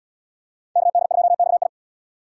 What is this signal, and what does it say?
Morse code, heard as a keyed tone, about 69 wpm, 700 Hz. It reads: KD2XI